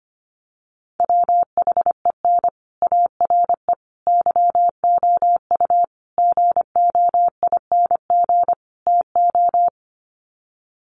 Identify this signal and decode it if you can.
Morse code, heard as a keyed tone, about 25 wpm, 700 Hz. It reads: WHEN ARE YOU GOING TO
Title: Sets of 5 Words Encore2 25wpm